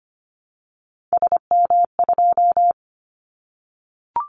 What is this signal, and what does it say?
Morse code, heard as a keyed tone, about 25 wpm, 700 Hz. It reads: SM2
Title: Callsign Prefixes for Top 25 DXCC - 25wpm